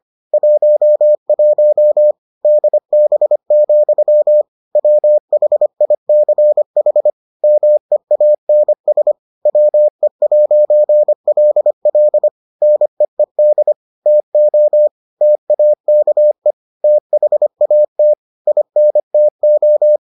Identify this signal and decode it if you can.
Morse code, heard as a keyed tone, about 25 wpm, 600 Hz. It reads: 11 DB, WHICH MEANS WE'LL NEED TO TAKE THAT INTO